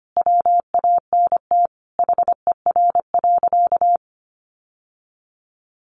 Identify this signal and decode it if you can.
Morse code, heard as a keyed tone, about 25 wpm, 700 Hz. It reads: WANT HER.
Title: Sentences from Top 800 Words 25wpm